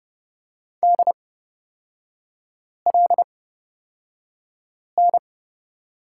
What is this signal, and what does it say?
Morse code, heard as a keyed tone, about 29 wpm, spaced as for 5 wpm, 700 Hz. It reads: DLN
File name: Lesson 38e - Letter q - Three Characters 30wpm